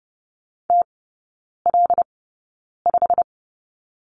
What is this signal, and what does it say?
Morse code, heard as a keyed tone, about 30 wpm, spaced as for 9 wpm, 700 Hz. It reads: TL5
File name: Lesson 15e - Number 5 - Three Characters 30wpm